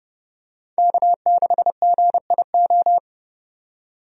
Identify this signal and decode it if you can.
Morse code, heard as a keyed tone, about 30 wpm, 700 Hz. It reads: K6GIO
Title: USA Call Signs - 5 Characters - 30wpm